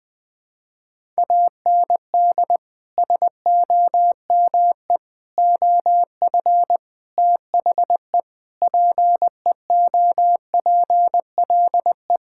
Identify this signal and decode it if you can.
Morse code, heard as a keyed tone, about 20 wpm, 700 Hz. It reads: AND SOME OF THE PEOPLE
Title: Sentences with 4-Word Phrases - 20wpm